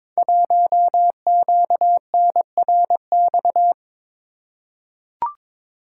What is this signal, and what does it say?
Morse code, heard as a keyed tone, about 23 wpm, 700 Hz. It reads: 1QNRX
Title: Code Groups - 5 Letter:Numbers - 22wpm